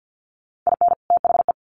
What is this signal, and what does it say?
Morse code, heard as a keyed tone, about 48 wpm, 700 Hz. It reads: IN THE
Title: Sets of 3 Words 50wpm